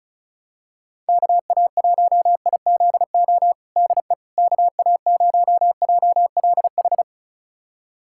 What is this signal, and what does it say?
Morse code, heard as a keyed tone, about 34 wpm, 700 Hz. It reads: KA1IZO DE KA0JLH
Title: USA Callsign DE Callsign - 6 Characters - 35wpm